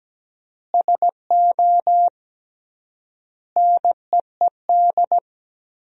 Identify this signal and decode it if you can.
Morse code, heard as a keyed tone, about 17 wpm, 700 Hz. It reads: SO NEED
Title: 3x Word-Spacing - Sentences from Top 200 Words - 17wpm